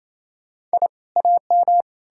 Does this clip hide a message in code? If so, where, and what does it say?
Yes: Morse code from 0.7 to 1.8 s, heard as a keyed tone, about 37 wpm, 700 Hz. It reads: I AM